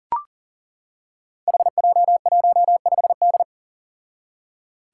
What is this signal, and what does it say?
Morse code, heard as a keyed tone, about 37 wpm, 700 Hz. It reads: HJ15D
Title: Five Challenging Letter-Number Combinations 40wpm